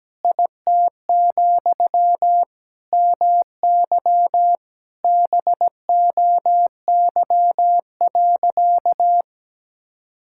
Morse code, heard as a keyed tone, about 17 wpm, 700 Hz. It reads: IT, MY BOY.
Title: Sentences from Top 1000 Words - Review Part 5 - 17wpm